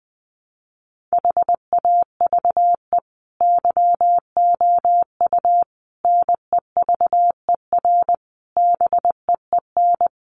Morse code, heard as a keyed tone, about 20 wpm, 700 Hz. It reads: HAVE YOU NEVER BEEN
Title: Sentences from Top 1400 Words 20wpm